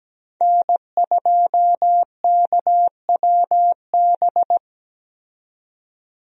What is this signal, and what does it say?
Morse code, heard as a keyed tone, about 17 wpm, 700 Hz. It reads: N2KWB
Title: USA Call Signs - 5 Characters - 17wpm